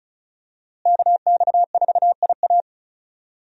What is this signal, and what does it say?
Morse code, heard as a keyed tone, about 36 wpm, 700 Hz. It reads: KX4IA
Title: USA Call Signs - 5 Characters - Repeated 3x - 35wpm